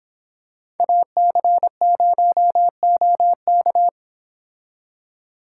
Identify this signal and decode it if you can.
Morse code, heard as a keyed tone, about 25 wpm, 700 Hz. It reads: AC0OK